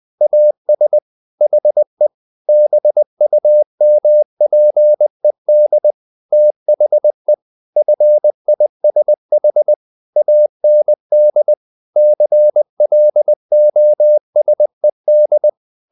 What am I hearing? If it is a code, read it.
Morse code, heard as a keyed tone, about 20 wpm, 600 Hz. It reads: AS HE BUMPED THE FISH AND CLOSED
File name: the old man and the sea part 2 20wpm